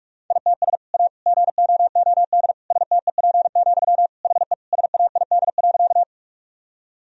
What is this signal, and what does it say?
Morse code, heard as a keyed tone, about 41 wpm, 700 Hz. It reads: ITS A GOOD STEP, HE SAID.
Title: Sentences from Top 1000 Words - Review Part 2 - 45wpm